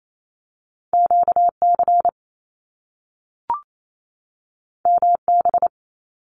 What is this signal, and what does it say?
Morse code, heard as a keyed tone, about 28 wpm, 700 Hz. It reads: QC MB
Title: Canadian Provinces and Territories 28wpm